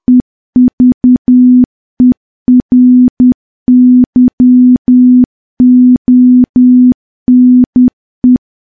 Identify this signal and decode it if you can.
Morse code, heard as a keyed tone, about 10 wpm, 250 Hz. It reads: EVERYONE
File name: morse / morse-training-10WPM